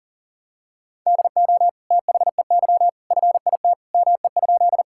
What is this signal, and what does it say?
Morse code, heard as a keyed tone, about 40 wpm, 700 Hz. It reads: DO THEY FIT ME?
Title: Sentences from Top 1000 Words - Review Part 5 - 40wpm